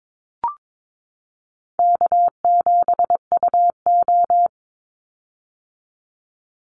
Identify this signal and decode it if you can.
Morse code, heard as a keyed tone, about 22 wpm, 700 Hz. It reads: K7UO
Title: USA Call Signs 22wpm